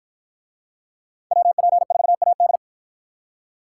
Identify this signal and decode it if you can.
Morse code, heard as a keyed tone, about 53 wpm, 700 Hz. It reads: WP4AD